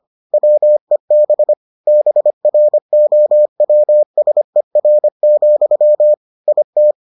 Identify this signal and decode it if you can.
Morse code, heard as a keyed tone, about 25 wpm, 600 Hz. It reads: WEB BROWSER, IT